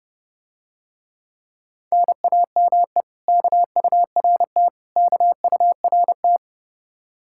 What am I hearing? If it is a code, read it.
Morse code, heard as a keyed tone, about 30 wpm, 700 Hz. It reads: NAME KURT KURT